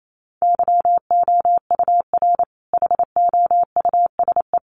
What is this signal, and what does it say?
Morse code, heard as a keyed tone, about 28 wpm, 700 Hz. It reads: YOUR HOUSE
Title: Sentences from Top 1300 Words 28wpm